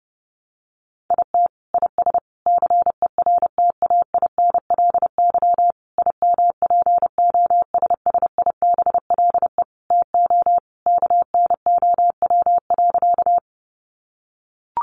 Morse code, heard as a keyed tone, about 30 wpm, 700 Hz. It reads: IT IS CERTAINLY IMPOSSIBLE TO KNOW.
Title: Sentences from Top 1100 Words 30wpm